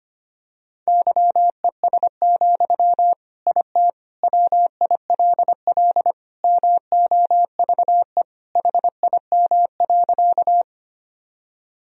Morse code, heard as a keyed tone, about 25 wpm, 700 Hz. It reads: YES, IT WILL MOVE HIM.
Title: Sentences from Top 1000 Words - Review Part 1 - 25wpm